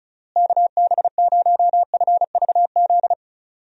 Morse code, heard as a keyed tone, about 36 wpm, 700 Hz. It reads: KB0FVZ
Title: USA Call Signs - 6 Characters - Repeated 3x - 35wpm